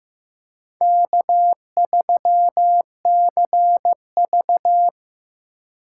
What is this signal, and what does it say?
Morse code, heard as a keyed tone, about 15 wpm, 700 Hz. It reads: K3CV
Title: USA Call Signs - 4 Characters - 2x Character Spacing - 15wpm